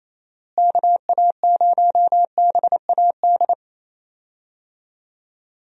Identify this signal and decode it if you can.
Morse code, heard as a keyed tone, about 27 wpm, 700 Hz. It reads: KA0BAD
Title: USA Call Signs - 6 Characters - Repeated 2x - 28wpm